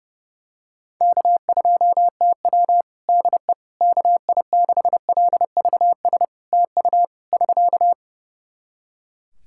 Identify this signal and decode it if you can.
Morse code, heard as a keyed tone, about 30 wpm, 700 Hz. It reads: K2TW DE KI6LVS TU <SK>
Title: Practice QSO - Exchange 4 - Ending - 30wpm